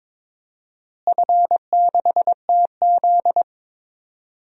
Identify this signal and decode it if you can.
Morse code, heard as a keyed tone, about 22 wpm, 700 Hz. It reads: F6TZ
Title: DXCC-100 Call Signs - 4 Characters - Repeated 3x - 22wpm